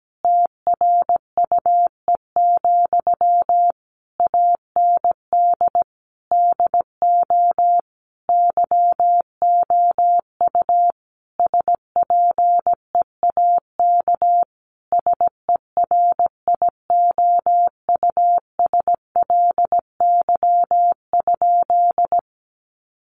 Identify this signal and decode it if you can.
Morse code, heard as a keyed tone, about 17 wpm, 700 Hz. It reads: TRUE, AND DO YOU SPEAK SERIOUSLY?
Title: Sentences from Top 1200 Words 17wpm